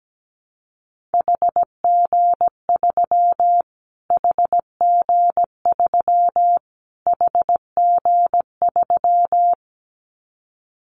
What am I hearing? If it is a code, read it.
Morse code, heard as a keyed tone, about 17 wpm, 700 Hz. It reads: HG3 HG3 HG3
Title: Callsign Prefixes for Top 25 DXCC - Repeated 3x - 17wpm